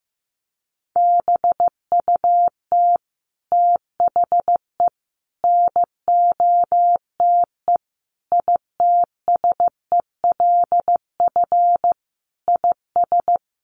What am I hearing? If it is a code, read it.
Morse code, heard as a keyed tone, about 15 wpm, 700 Hz. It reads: BUT THE NOTE ITSELF IS
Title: Sentences from Top 800 Words 15wpm